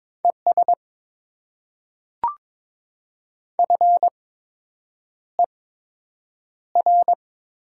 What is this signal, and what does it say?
Morse code, heard as a keyed tone, about 22 wpm, spaced as for 8 wpm, 700 Hz. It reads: ES FER
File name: Top 20 QSO Elements - 8x Character Spacing - 22wpm